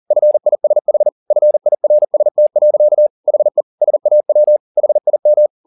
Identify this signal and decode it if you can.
Morse code, heard as a keyed tone, about 40 wpm, 600 Hz. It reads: FISH FIRST. HE SAW HIM
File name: the old man and the sea 2 40wpm